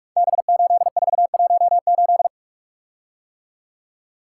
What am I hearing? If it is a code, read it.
Morse code, heard as a keyed tone, about 44 wpm, 700 Hz. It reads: B8418
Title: Code Groups - 5 Letter:Numbers - 45wpm